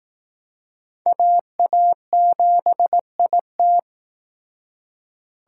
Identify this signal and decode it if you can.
Morse code, heard as a keyed tone, about 18 wpm, 700 Hz. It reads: AA7IT
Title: Speed Racing - USA 2x2 Call Signs 15wpm